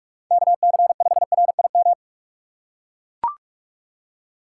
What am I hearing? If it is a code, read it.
Morse code, heard as a keyed tone, about 44 wpm, 700 Hz. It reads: KC5RIM
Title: USA Call Signs - 3x Character Spacing - 45wpm